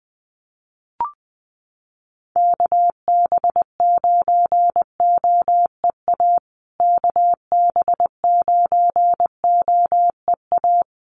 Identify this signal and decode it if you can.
Morse code, heard as a keyed tone, about 20 wpm, 700 Hz. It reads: KB9OEA KB9OEA
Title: USA Call Signs Repeated 2x 20wpm